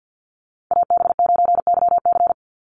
Morse code, heard as a keyed tone, about 49 wpm, 700 Hz. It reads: AB9YC
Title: Call Signs - 50wpm